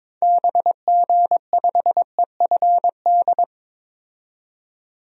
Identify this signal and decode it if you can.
Morse code, heard as a keyed tone, about 22 wpm, 700 Hz. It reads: BG5EFD
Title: DXCC-100 Call Signs - 6 Characters - 22wpm